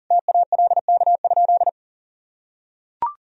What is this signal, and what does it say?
Morse code, heard as a keyed tone, about 37 wpm, 700 Hz. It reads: TALK?
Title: Sentences from Top 1000 Words - Review Part 1 - 40wpm